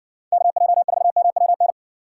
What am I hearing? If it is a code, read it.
Morse code, heard as a keyed tone, about 62 wpm, 700 Hz. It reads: KJ4MWN